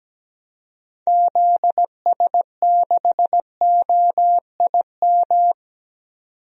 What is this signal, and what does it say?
Morse code, heard as a keyed tone, about 17 wpm, 700 Hz. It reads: ZS6OIM
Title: DXCC-100 Call Signs - 6 Characters - 17wpm